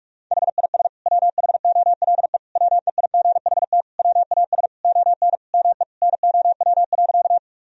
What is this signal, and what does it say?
Morse code, heard as a keyed tone, about 46 wpm, 700 Hz. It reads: HIS WHOLE WEIGHT WAS ON ME NOW.
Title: Sentences from Top 1000 Words - Review Part 3 - 45wpm